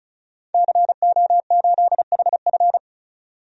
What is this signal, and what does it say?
Morse code, heard as a keyed tone, about 36 wpm, 700 Hz. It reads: CO8HF
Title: DXCC-100 Call Signs - 5 Characters - 35wpm